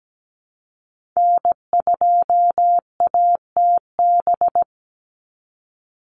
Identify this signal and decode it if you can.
Morse code, heard as a keyed tone, about 17 wpm, 700 Hz. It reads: N2ATB
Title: Call Signs Repeated 3x - 17wpm